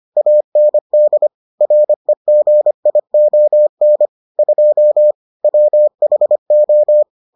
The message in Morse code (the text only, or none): AND REGION 2 WHO